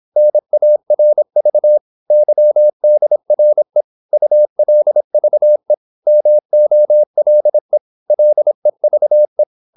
Morse code, heard as a keyed tone, about 26 wpm, 600 Hz. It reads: NARV YDRE ULVE MOLE LEVE
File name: Korte ord DK 26wpm